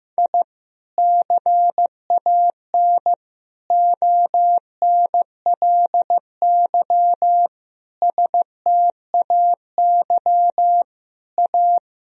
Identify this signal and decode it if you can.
Morse code, heard as a keyed tone, about 15 wpm, 700 Hz. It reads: I CAN ONLY STAY A
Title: Sentences from Top 1000 Words - Review Part 3 - 15wpm